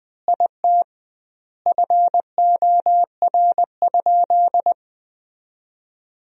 Morse code, heard as a keyed tone, about 21 wpm, 700 Hz. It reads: IT FOR?
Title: 2x Word-Spacing - Sentences from Top 100 Words - 20wpm